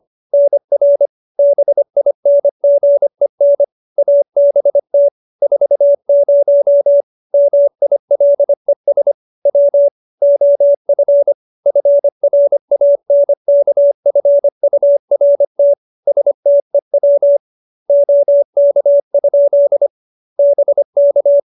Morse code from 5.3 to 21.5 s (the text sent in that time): 40 MILES W OF FRANKFURT STEW OK? BK